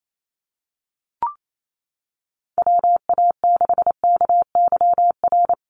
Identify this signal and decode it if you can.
Morse code, heard as a keyed tone, about 28 wpm, 700 Hz. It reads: WA6KYR
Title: NA SOTA Chaser Callsigns 28wpm